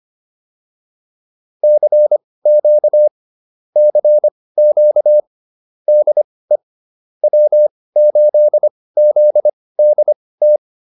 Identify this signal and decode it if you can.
Morse code, heard as a keyed tone, about 25 wpm, spaced as for 18 wpm, 600 Hz. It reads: CQ CQ DE W8ZDT